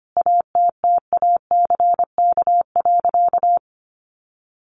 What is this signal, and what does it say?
Morse code, heard as a keyed tone, about 25 wpm, 700 Hz. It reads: ATTACK.
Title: Sentences from Top 1500 Words 25wpm